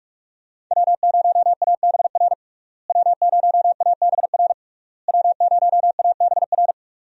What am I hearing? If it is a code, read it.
Morse code, heard as a keyed tone, about 42 wpm, 700 Hz. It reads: W0ABR W0ABR W0ABR
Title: USA Call Signs - 5 Characters - Repeated 3x - 45wpm